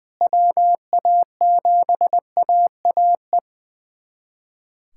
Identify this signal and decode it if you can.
Morse code, heard as a keyed tone, about 20 wpm, 700 Hz. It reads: WA7AAE
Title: USA Callsign DE Callsign - 6 Characters - 20wpm